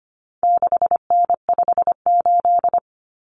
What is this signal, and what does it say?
Morse code, heard as a keyed tone, about 25 wpm, 700 Hz. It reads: 6N58
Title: Common Cut Numbers - 4-Digit 25wpm